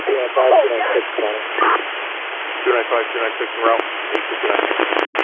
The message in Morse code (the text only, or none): none